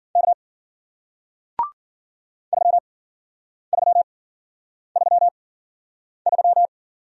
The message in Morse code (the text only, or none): K 3333